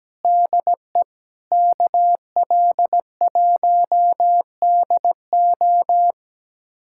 Morse code, heard as a keyed tone, about 17 wpm, 700 Hz. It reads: DE KL1DO
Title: USA Callsign DE Callsign - 5 Characters - 17wpm